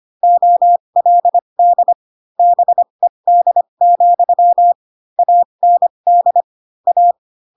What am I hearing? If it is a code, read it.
Morse code, heard as a keyed tone, about 25 wpm, spaced as for 22 wpm, 700 Hz. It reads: OLD BED, AND A
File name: Pinocchio / Pinocchio ch3 18 25 ik1hge